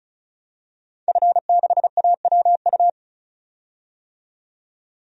Morse code, heard as a keyed tone, about 36 wpm, 700 Hz. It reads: F6AWU
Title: DXCC-100 Call Signs - 5 Characters - 35wpm